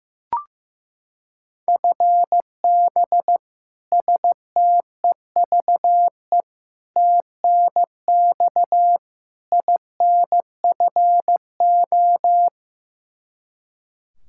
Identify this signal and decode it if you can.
Morse code, heard as a keyed tone, about 15 wpm, 700 Hz. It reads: FB STEVE TNX INFO
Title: Practice QSO - Exchange 3 - More Info - 15wpm